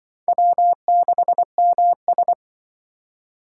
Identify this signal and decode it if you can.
Morse code, heard as a keyed tone, about 25 wpm, 700 Hz. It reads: W6MS
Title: Speed Racing - USA Call Signs 20wpm